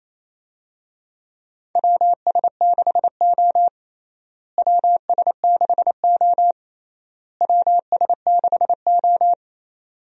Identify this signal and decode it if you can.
Morse code, heard as a keyed tone, about 27 wpm, 700 Hz. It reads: WS6O WS6O WS6O
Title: USA Call Signs - 4 Characters - Repeated 3x - 28wpm